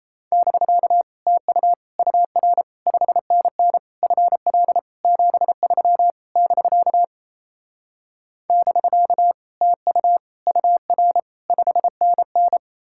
Morse code, heard as a keyed tone, about 38 wpm, 700 Hz. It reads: <BK> TU UR 5NN FL 73 <BK> <BK> TU UR 5NN